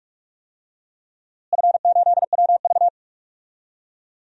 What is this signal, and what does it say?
Morse code, heard as a keyed tone, about 44 wpm, 700 Hz. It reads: F8WV
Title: DXCC-100 Call Signs - 4 Characters - Repeated 3x - 45wpm